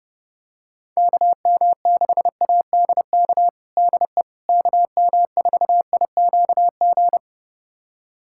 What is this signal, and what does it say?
Morse code, heard as a keyed tone, about 32 wpm, 700 Hz. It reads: KM6ADK DE KM4IQG
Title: USA Callsign DE Callsign - 6 Characters - 30wpm